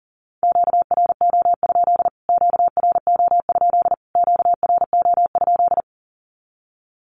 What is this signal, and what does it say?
Morse code, heard as a keyed tone, about 40 wpm, 700 Hz. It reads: QRO? QRO? QRO?
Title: Q-Codes Repeated 3x 40wpm